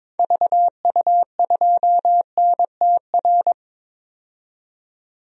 Morse code, heard as a keyed tone, about 22 wpm, 700 Hz. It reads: VU2NTR
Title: DXCC-100 Call Signs - 6 Characters - 22wpm